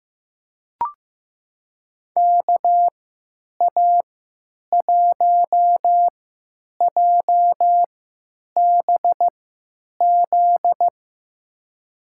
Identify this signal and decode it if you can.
Morse code, heard as a keyed tone, about 15 wpm, spaced as for 9 wpm, 700 Hz. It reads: KA1JBZ
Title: USA Call Signs - 6 Characters - 3x Character Spacing - 15wpm